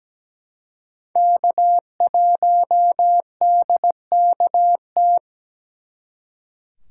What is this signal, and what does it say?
Morse code, heard as a keyed tone, about 20 wpm, 700 Hz. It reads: K1DKT
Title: Speed Racing - USA Call Signs 17wpm